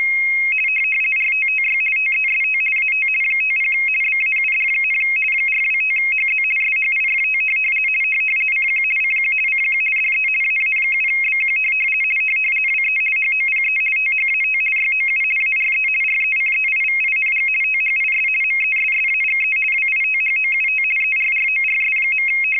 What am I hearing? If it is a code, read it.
RTTY, about 50 baud. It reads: RTTY TEST AT 50 BAUD, 425HZ SHIFT CQ CQ CQ DE AA7BQ WE HOPE THAT YOU ARE RECEIVING THIS MESSAGE WITHOUT ERRORS.
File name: rtty425 / rtty50.wav / rtty50